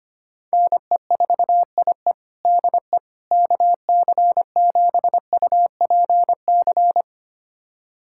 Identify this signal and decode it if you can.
Morse code, heard as a keyed tone, about 25 wpm, 700 Hz. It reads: NE4IE DE KC7UPC
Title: USA Callsign DE Callsign 25wpm